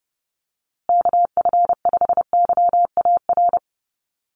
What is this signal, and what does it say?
Morse code, heard as a keyed tone, about 30 wpm, 700 Hz. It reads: KF5YAR
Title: USA Call Signs Repeated 3x 30wpm